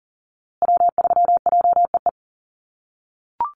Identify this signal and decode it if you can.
Morse code, heard as a keyed tone, about 40 wpm, 700 Hz. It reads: W3JEE
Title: USA Call Signs Repeated 2x 40wpm